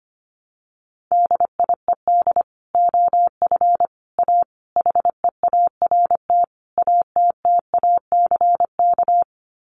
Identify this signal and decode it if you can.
Morse code, heard as a keyed tone, about 25 wpm, 700 Hz. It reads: DIED OF A HEART ATTACK
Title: Sets of 5 Words Encore2 25wpm